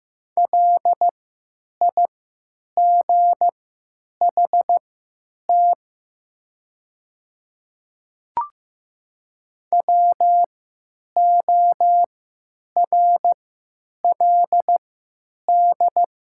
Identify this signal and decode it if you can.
Morse code, heard as a keyed tone, about 15 wpm, spaced as for 9 wpm, 700 Hz. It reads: LIGHT WORLD
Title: 3x Character Spacing - Top 300 Words - 15wpm